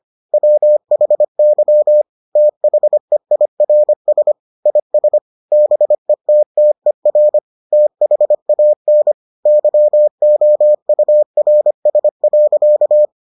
Morse code, heard as a keyed tone, about 25 wpm, 600 Hz. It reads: WHY THEIRS IS BETTER THAN YOURS.